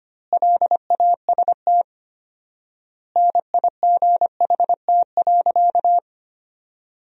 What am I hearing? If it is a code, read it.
Morse code, heard as a keyed tone, about 25 wpm, 700 Hz. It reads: LAST NIGHT.
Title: 4x Word-Spacing - Sentences from Top 300 Words - 25wpm